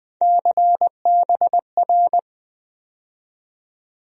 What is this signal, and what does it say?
Morse code, heard as a keyed tone, about 20 wpm, 700 Hz. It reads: CBR
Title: USA Call Sign Postfixes - 3 Letters 20wpm